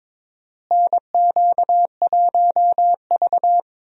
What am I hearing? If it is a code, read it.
Morse code, heard as a keyed tone, about 22 wpm, 700 Hz. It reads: NQ1V
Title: USA Call Signs - 4 Characters - Repeated 3x - 22wpm